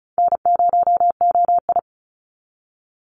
Morse code, heard as a keyed tone, about 34 wpm, 700 Hz. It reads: N0OI
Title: NA SOTA Activator Callsigns 35wpm